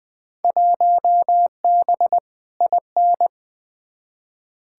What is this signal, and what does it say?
Morse code, heard as a keyed tone, about 20 wpm, 700 Hz. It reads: 1B IN
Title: Field Day Exchanges - 20wpm